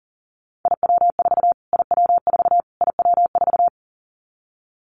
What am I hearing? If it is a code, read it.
Morse code, heard as a keyed tone, about 40 wpm, 700 Hz. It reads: IW4 IW4 IW4
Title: Callsign Prefixes for Top 25 DXCC - Repeated 3x - 40wpm